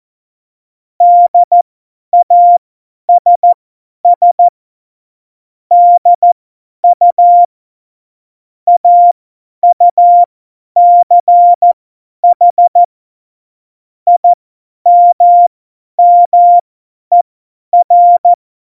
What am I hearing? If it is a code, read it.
Morse code, heard as a keyed tone, about 14 wpm, spaced as for 10 wpm, 700 Hz. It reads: DASS DU AUCH IMMER
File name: Morsekurs / Witze / Witz 24.Dezember 2019, 50 F70